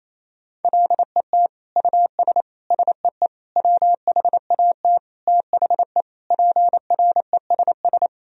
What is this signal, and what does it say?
Morse code, heard as a keyed tone, about 28 wpm, 700 Hz. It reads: LET US SEE WHAT THE PRESS
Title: Sentences from Top 1000 Words - Review Part 1 - 28wpm